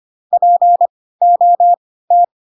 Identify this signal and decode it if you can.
Morse code, heard as a keyed tone, about 25 wpm, spaced as for 16 wpm, 700 Hz. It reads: POT